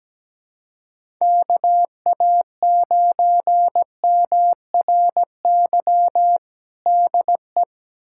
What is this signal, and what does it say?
Morse code, heard as a keyed tone, about 17 wpm, 700 Hz. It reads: KA9MRY DE
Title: USA Callsign DE Callsign - 6 Characters - 17wpm